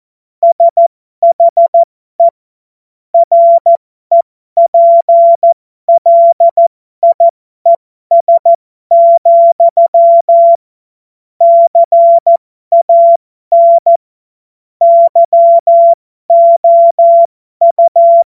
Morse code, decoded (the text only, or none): SHE REPLIES, CAN YOU